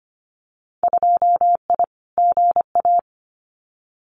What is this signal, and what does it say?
Morse code, heard as a keyed tone, about 25 wpm, 690 Hz. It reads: 2I GA
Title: Winter Field Day Exchanges 25wpm